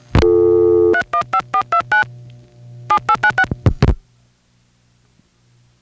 DTMF, heard as telephone touch tones, tones of about 80 ms. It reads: 315429 7896